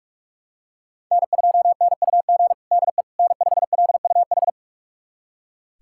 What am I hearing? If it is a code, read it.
Morse code, heard as a keyed tone, about 46 wpm, 700 Hz. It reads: N2NUG DE N5LUH
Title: USA Callsign DE Callsign - 5 Characters - 45wpm